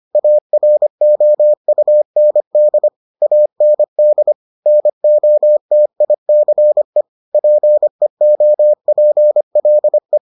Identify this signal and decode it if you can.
Morse code, heard as a keyed tone, about 25 wpm, 600 Hz. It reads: AROUND AND NOTICE PEOPLE